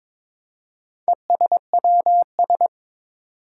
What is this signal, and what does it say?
Morse code, heard as a keyed tone, about 22 wpm, 700 Hz. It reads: ESWS
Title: Four Challenging Letter-Number Combinations 22wpm